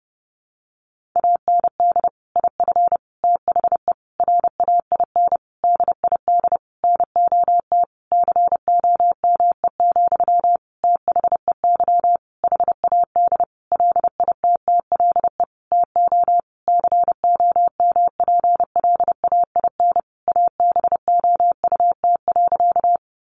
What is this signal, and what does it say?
Morse code, heard as a keyed tone, about 30 wpm, 700 Hz. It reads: AND IF THE RAIN DID NOT COME, THEY HAD LITTLE TO COMPLAIN ABOUT.
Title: Sentences from Top 1300 Words 30wpm